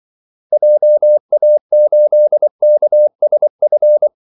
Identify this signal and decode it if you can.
Morse code, heard as a keyed tone, about 24 wpm, 600 Hz. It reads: JA8KSF
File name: Callsigns-24wpm